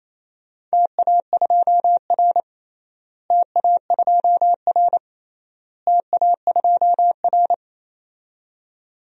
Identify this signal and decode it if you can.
Morse code, heard as a keyed tone, about 28 wpm, 700 Hz. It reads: TA2R TA2R TA2R
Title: DXCC-100 Call Signs - 4 Characters - Repeated 3x - 28wpm